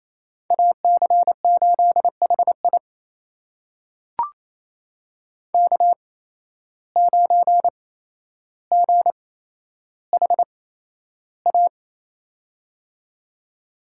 Morse code, heard as a keyed tone, about 28 wpm, spaced as for 12 wpm, 700 Hz. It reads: AC8HI K9GHA
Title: USA Call Signs - 5 Characters - 8x Character Spacing - 28wpm